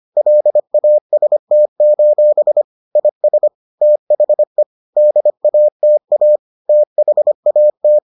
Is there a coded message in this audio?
Yes: Morse code from 0.2 to 8.0 s, heard as a keyed tone, about 25 wpm, 600 Hz. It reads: LAST: IS THE DATA THAT